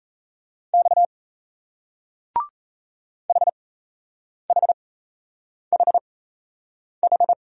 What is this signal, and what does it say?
Morse code, heard as a keyed tone, about 37 wpm, spaced as for 8 wpm, 700 Hz. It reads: X HHHH